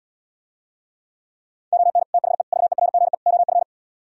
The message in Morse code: ON THE VERGE OF

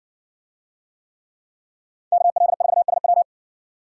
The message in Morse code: KL4SW

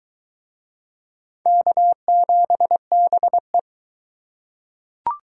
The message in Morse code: K7BE

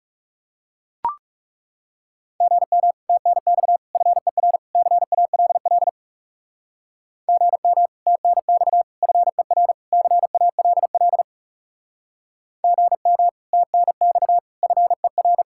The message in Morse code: GM TNX FER CALL GM TNX FER CALL GM TNX FER

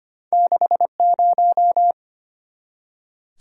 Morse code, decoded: 6 0

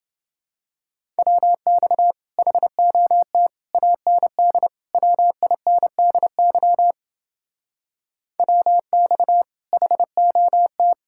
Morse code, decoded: WX HOT AND WINDY WX HOT

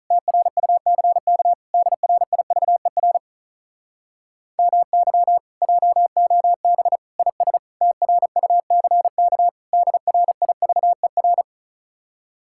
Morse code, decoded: TRUCK DRIVER MY JOB IS TRUCK DRIVER

